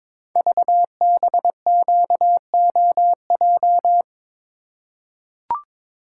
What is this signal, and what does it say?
Morse code, heard as a keyed tone, about 22 wpm, 700 Hz. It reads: VBQOJ
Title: Code Groups - 5 Letters - 22wpm